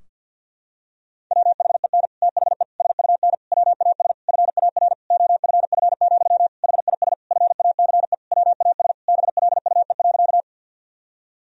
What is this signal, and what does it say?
Morse code, heard as a keyed tone, about 52 wpm, 700 Hz. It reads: WHEN THE SUN WAS FAR OFF, HIS FACE WAS BLUE.
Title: Sentences from Top 1000 Words - Review Part 3 - 50wpm